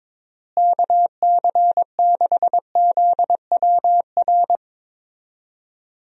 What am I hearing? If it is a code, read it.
Morse code, heard as a keyed tone, about 22 wpm, 700 Hz. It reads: KC6ZWR
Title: USA Call Signs - 6 Characters - Repeated 3x - 22wpm